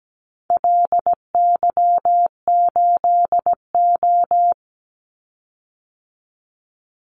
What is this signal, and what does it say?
Morse code, heard as a keyed tone, about 17 wpm, 700 Hz. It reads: LY8O
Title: Call Signs Repeated 2x - 17wpm